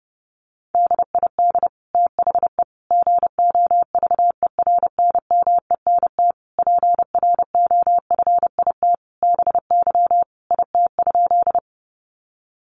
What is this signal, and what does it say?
Morse code, heard as a keyed tone, about 30 wpm, 700 Hz. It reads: DID THE GOVERNMENT PROFIT BY IT?
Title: Sentences from Top 800 Words 30wpm